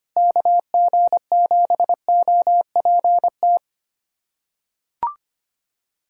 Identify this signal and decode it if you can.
Morse code, heard as a keyed tone, about 25 wpm, 700 Hz. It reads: KG7OPT
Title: USA Callsign DE Callsign - 6 Characters - 25wpm